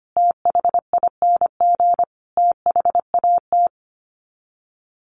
Morse code, heard as a keyed tone, about 25 wpm, 700 Hz. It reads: THING THAT
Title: Sets of 5 Words Encore 25wpm